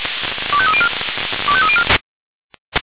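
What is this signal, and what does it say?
Five-tone selective call (ZVEI), heard as quick runs of five tones, tones of about 70 ms. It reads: 25304 25304
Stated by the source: ZVEI five-tone call received on SDR.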